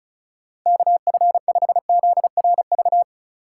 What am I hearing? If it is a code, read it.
Morse code, heard as a keyed tone, about 36 wpm, 700 Hz. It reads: KF5ZRV
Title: USA Call Signs - 6 Characters - 2x Character Spacing - 35wpm